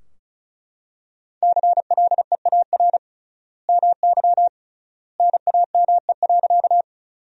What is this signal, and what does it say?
Morse code, heard as a keyed tone, about 34 wpm, 700 Hz. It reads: CLEAR MY NAME.
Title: 3x Word-Spacing - Sentences from Top 500 Words - 35wpm